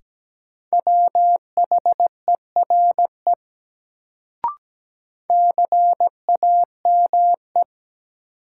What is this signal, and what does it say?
Morse code, heard as a keyed tone, about 17 wpm, 700 Hz. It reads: WHERE CAME
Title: Sentences from Top 200 Words - New Word Drill - 17wpm